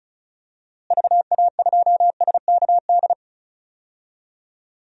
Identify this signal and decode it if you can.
Morse code, heard as a keyed tone, about 34 wpm, 700 Hz. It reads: VA2SKD
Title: DXCC-100 Call Signs - 6 Characters - Repeated 2x - 35wpm